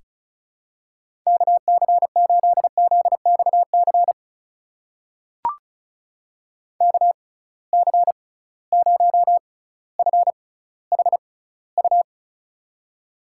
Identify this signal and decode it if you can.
Morse code, heard as a keyed tone, about 36 wpm, spaced as for 18 wpm, 700 Hz. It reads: KC8ZXC KC0FHU
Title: USA Call Signs - 6 Characters - 6x Character Spacing - 35wpm